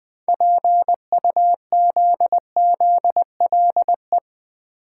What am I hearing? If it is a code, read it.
Morse code, heard as a keyed tone, about 21 wpm, 700 Hz. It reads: PUZZLE
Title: 6 Letter Words - First 3 Letters Given - 20wpm